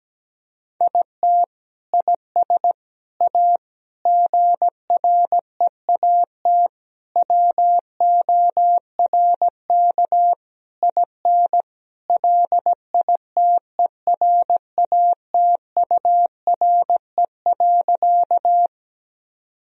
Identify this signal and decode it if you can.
Morse code, heard as a keyed tone, about 17 wpm, 700 Hz. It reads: IT IS A GREAT WORK IN LITERATURE.
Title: Sentences from Top 1000 Words - Review Part 3 - 17wpm